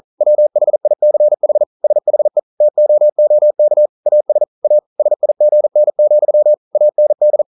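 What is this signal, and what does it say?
Morse code, heard as a keyed tone, about 41 wpm, 600 Hz. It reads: WHICH SHE TOOK AS A SIGN, AND
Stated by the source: Morse code practice at 41 wpm, from CW med Gnister.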